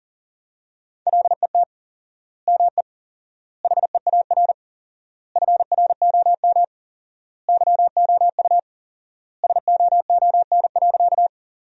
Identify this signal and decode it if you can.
Morse code, heard as a keyed tone, about 37 wpm, 700 Hz. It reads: LET ME HEAR FROM YOU SOON.
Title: 4x Word-Spacing - Sentences from Top 400 Words - 40wpm